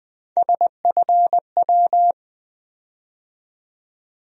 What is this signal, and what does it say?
Morse code, heard as a keyed tone, about 20 wpm, 700 Hz. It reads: SFW
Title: USA Call Sign Postfixes - 3 Letters 20wpm